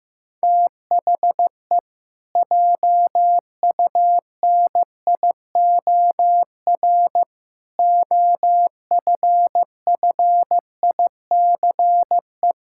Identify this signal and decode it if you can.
Morse code, heard as a keyed tone, about 15 wpm, 700 Hz. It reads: THE JUNIOR OFFICE
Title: Pangram Challenge v2 - 15wpm